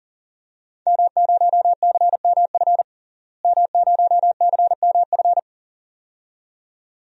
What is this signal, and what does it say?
Morse code, heard as a keyed tone, about 40 wpm, 700 Hz. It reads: M0CMF M0CMF
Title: DXCC-100 Call Signs - 5 Characters - Repeated 2x - 40wpm